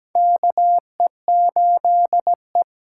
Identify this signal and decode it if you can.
Morse code, heard as a keyed tone, about 17 wpm, 700 Hz. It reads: KE8E
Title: USA Call Signs - 8x Character Spacing - 17wpm